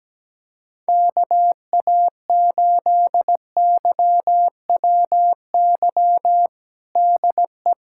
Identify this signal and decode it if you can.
Morse code, heard as a keyed tone, about 17 wpm, 700 Hz. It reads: KA8YWY DE
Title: USA Callsign DE Callsign - 6 Characters - 17wpm